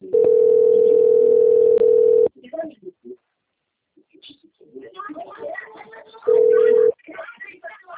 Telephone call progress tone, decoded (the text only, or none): ringback tone